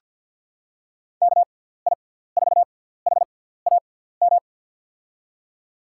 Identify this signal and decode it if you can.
Morse code, heard as a keyed tone, about 46 wpm, spaced as for 17 wpm, 700 Hz. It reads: KI4HAM
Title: USA Call Signs - 6x Character Spacing - 50wpm